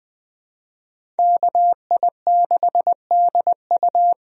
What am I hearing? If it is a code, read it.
Morse code, heard as a keyed tone, about 21 wpm, 700 Hz. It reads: KI6DU
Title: USA Call Signs - 5 Characters - Repeated 2x - 20wpm